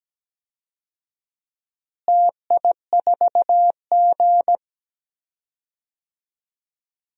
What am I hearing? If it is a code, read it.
Morse code, heard as a keyed tone, about 17 wpm, 700 Hz. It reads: TI4G
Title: DXCC-100 Call Signs - 4 Characters - Repeated 2x - 17wpm